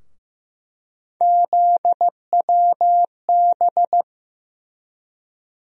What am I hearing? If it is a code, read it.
Morse code, heard as a keyed tone, about 15 wpm, 700 Hz. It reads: ZWB
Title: USA Call Sign Postfixes - 3 Letters 15wpm